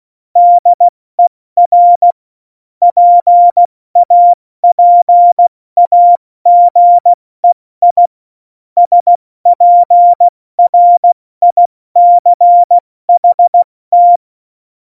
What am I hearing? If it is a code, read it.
Morse code, heard as a keyed tone, about 16 wpm, 700 Hz. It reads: DER PAPAGEI SPRICHT
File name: Morsekurs / Witze / Witz 23. Oktober 2018, 70 F80